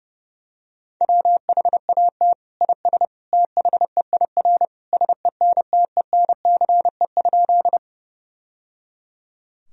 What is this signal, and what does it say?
Morse code, heard as a keyed tone, about 30 wpm, 700 Hz. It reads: WHAT IS THEIR SENTENCE?
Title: Sentences from Top 1000 Words - Review Part 1 - 30wpm